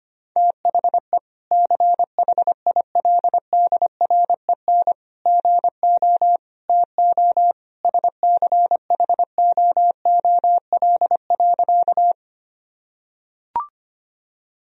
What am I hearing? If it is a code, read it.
Morse code, heard as a keyed tone, about 25 wpm, 700 Hz. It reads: THE CHILDREN GO TO SCHOOL.
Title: Sentences from Top 1000 Words - Review Part 1 - 25wpm